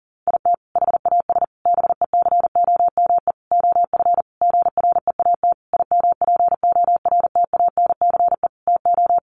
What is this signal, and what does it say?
Morse code, heard as a keyed tone, about 40 wpm, 700 Hz. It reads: IT HAS BECOME OF GREAT IMPORTANCE TO
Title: Sentences from Top 1200 Words 40wpm